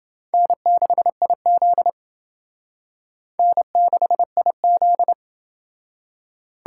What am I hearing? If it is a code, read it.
Morse code, heard as a keyed tone, about 29 wpm, 700 Hz. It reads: N6IZ N6IZ